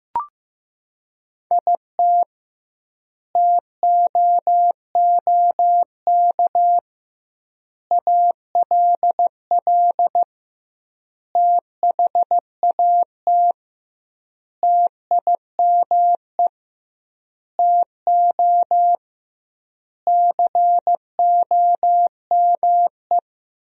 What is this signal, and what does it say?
Morse code, heard as a keyed tone, about 15 wpm, 700 Hz. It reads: IT TOOK ALL THAT TIME TO COME
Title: 2x Word-Spacing - Sentences from Top 300 Words - 15wpm